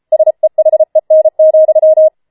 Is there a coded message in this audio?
Yes: Morse code from 0.1 to 2.1 s, heard as a keyed tone, about 33 wpm, 630 Hz. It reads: SEHEN,